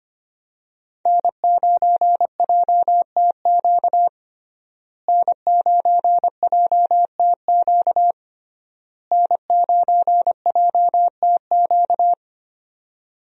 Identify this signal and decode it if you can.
Morse code, heard as a keyed tone, about 25 wpm, 700 Hz. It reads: N9JTQ N9JTQ N9JTQ
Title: USA Call Signs - 5 Characters - Repeated 3x - 25wpm